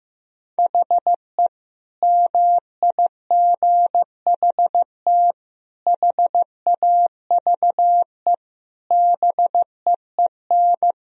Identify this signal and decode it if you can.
Morse code, heard as a keyed tone, about 15 wpm, 700 Hz. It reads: HE MIGHT HAVE BEEN
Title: Sentences from Top 1000 Words - Review Part 2 - 15wpm